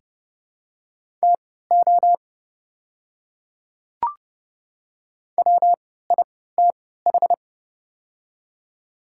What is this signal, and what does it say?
Morse code, heard as a keyed tone, about 30 wpm, spaced as for 17 wpm, 700 Hz. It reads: TO WITH
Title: 3x Character Spacing - Top 100 Words - 30wpm